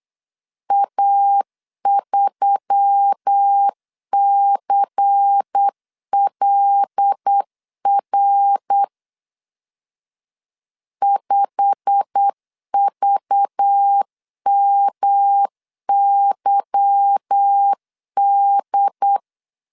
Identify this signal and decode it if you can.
Morse code, heard as a keyed tone, about 8 wpm, 800 Hz. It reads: A3CLR 5VMYD
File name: LEZIONI / 8 YJ45 / 8 YJ45 8wpm 3214